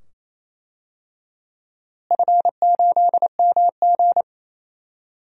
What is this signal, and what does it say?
Morse code, heard as a keyed tone, about 28 wpm, 700 Hz. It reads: F8MG
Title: DXCC-100 Call Signs - 4 Characters - Repeated 2x - 28wpm